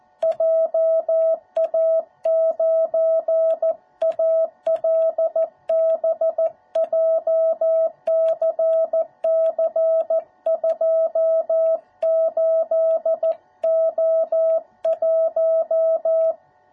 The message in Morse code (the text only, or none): JA9ALB JCC28O1